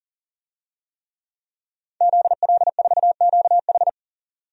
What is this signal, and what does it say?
Morse code, heard as a keyed tone, about 40 wpm, 700 Hz. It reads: ZL4QH
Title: DXCC-100 Call Signs - 5 Characters - 40wpm